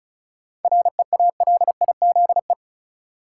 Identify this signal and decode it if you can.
Morse code, heard as a keyed tone, about 36 wpm, 700 Hz. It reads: REALIZE
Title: Sentences from Top 1000 Words - New Word Drill - 35wpm